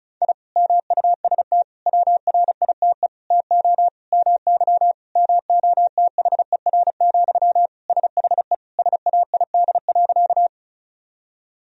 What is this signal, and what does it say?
Morse code, heard as a keyed tone, about 36 wpm, 700 Hz. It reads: I MUST WRITE TO MY MOTHER, SHE SAID.
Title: Sentences from Top 1000 Words - Review Part 1 - 35wpm